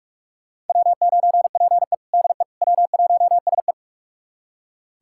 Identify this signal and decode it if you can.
Morse code, heard as a keyed tone, about 44 wpm, 690 Hz. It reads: W9PE DE W1SE
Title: USA Callsign DE Callsign - 4 Characters - 45wpm